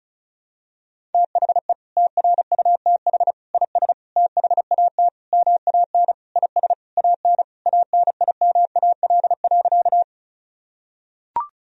Morse code, heard as a keyed tone, about 36 wpm, 700 Hz. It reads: THE TRUTH IS THAT MAN IS AN ANIMAL.
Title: Sentences from Top 1000 Words - Review Part 1 - 35wpm